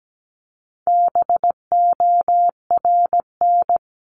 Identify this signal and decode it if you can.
Morse code, heard as a keyed tone, about 17 wpm, 700 Hz. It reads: BORN
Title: Top 751-1000 Words 17wpm